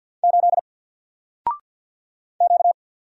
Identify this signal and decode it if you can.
Morse code, heard as a keyed tone, about 52 wpm, 700 Hz. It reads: 8 Q
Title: Single Letter-Number Alt 50wpm